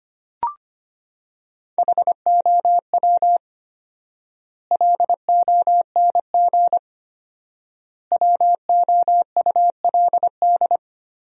Morse code, heard as a keyed tone, about 25 wpm, 700 Hz. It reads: HOW LONG WOULD
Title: 4x Word-Spacing - Sentences from Top 500 Words - 25wpm